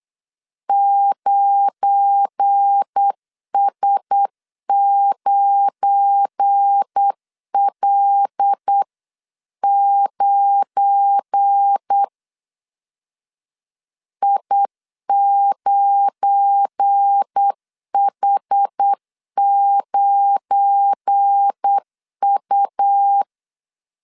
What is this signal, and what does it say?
Morse code, heard as a keyed tone, about 8 wpm, 800 Hz. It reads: 9S9L9 I9H9U
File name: LEZIONI / 7 GF9Z / 7 GF9 8wpm 3545